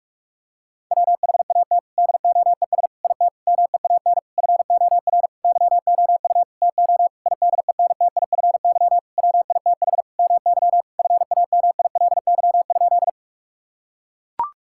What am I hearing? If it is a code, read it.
Morse code, heard as a keyed tone, about 41 wpm, 700 Hz. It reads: WHAT DOES IT MEAN FOR YOU TO IDENTIFY WITH MY FAMILY?
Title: Sentences from Top 1000 Words - Review Part 6 - 45wpm